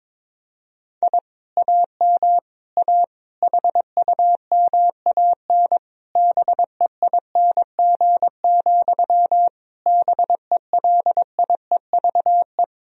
Morse code, heard as a keyed tone, about 22 wpm, 700 Hz. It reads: I AM A HUMAN BEING, BELIEVE
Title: Sentences from Top 1000 Words - Review Part 3 - 22wpm